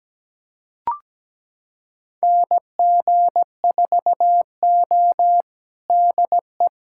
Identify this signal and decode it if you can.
Morse code, heard as a keyed tone, about 17 wpm, 700 Hz. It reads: NG4O DE
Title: USA Callsign DE Callsign - 4 Characters - 17wpm